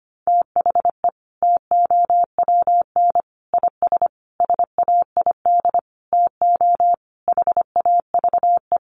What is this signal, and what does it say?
Morse code, heard as a keyed tone, about 25 wpm, 700 Hz. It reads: THE TOWN IS SAID TO HAVE
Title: Sentences from Top 1300 Words 25wpm